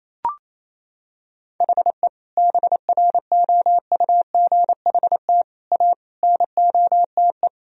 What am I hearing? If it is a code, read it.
Morse code, heard as a keyed tone, about 28 wpm, 700 Hz. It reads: HE BROUGHT A NOTE
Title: Sentences from Top 1000 Words - Review Part 2 - 28wpm